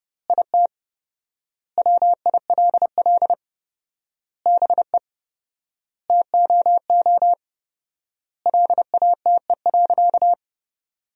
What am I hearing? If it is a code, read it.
Morse code, heard as a keyed tone, about 30 wpm, 700 Hz. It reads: IT WILL BE TOO LATE.
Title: 4x Word-Spacing - Sentences from Top 300 Words - 30wpm